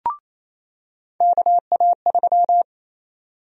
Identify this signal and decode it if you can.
Morse code, heard as a keyed tone, about 28 wpm, 700 Hz. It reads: KA3
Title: USA Call Sign Prefixes - 28wpm